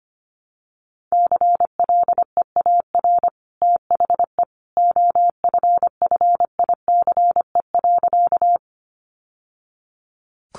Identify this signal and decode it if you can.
Morse code, heard as a keyed tone, about 25 wpm, 700 Hz. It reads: CLEAR THE OFFICE.
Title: Sentences from Top 1000 Words 25wpm